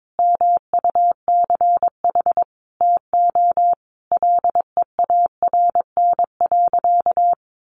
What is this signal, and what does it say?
Morse code, heard as a keyed tone, about 22 wpm, 700 Hz. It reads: MUCH TO LEARN.
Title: Sentences from Top 1400 Words 22wpm